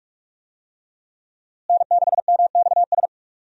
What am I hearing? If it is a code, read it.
Morse code, heard as a keyed tone, about 44 wpm, 700 Hz. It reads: N6MXS
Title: USA Call Signs - 5 Characters - Repeated 3x - 45wpm